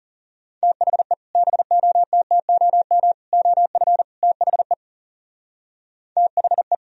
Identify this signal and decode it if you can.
Morse code, heard as a keyed tone, about 38 wpm, 700 Hz. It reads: THE BOTTOM OF THE THE